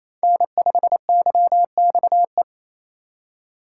Morse code, heard as a keyed tone, about 35 wpm, 700 Hz. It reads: N5YXE
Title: Speed Racing - USA 1x3 Call Signs 28wpm